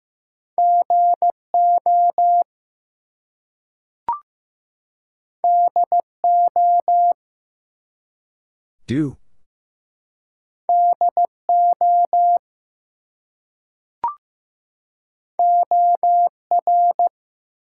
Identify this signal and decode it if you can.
Morse code, heard as a keyed tone, about 15 wpm, 700 Hz. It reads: GO DO DO OR
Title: Two Letter Words - 15wpm